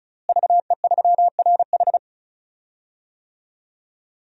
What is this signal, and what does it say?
Morse code, heard as a keyed tone, about 36 wpm, 700 Hz. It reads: VE3RH
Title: DXCC-100 Call Signs - 5 Characters - Repeated 2x - 35wpm